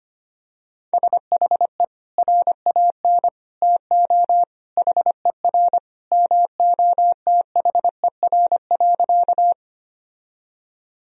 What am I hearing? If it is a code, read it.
Morse code, heard as a keyed tone, about 25 wpm, 700 Hz. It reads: SHE RAN TO HER MOTHER.
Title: Sentences from Top 1000 Words - Review Part 3 - 25wpm